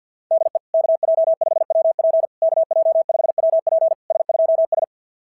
Morse code, heard as a keyed tone, about 48 wpm, 690 Hz. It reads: DE KJ5WP KJ5WP S2S
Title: SOTA QSO Practice 50wpm